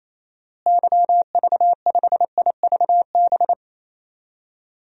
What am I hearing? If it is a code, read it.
Morse code, heard as a keyed tone, about 28 wpm, 700 Hz. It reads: YV5IVB
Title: DXCC-100 Call Signs - 6 Characters - Repeated 3x - 28wpm